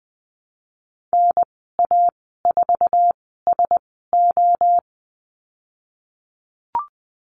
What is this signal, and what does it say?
Morse code, heard as a keyed tone, about 19 wpm, spaced as for 14 wpm, 700 Hz. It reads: NA4SO
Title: 048 Lesson 9g - Number 4 - Call Signs 20wpm